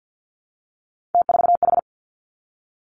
Morse code, heard as a keyed tone, about 50 wpm, 700 Hz. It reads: T4H
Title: Three Challenging Letter-Number Combinations 50wpm